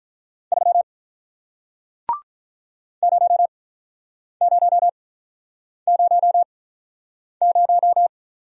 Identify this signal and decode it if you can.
Morse code, heard as a keyed tone, about 46 wpm, spaced as for 9 wpm, 700 Hz. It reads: 3 0000